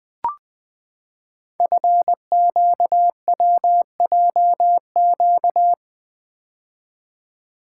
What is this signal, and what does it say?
Morse code, heard as a keyed tone, about 20 wpm, 700 Hz. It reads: FQWJQ
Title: Code Groups - 5 Letters - 20wpm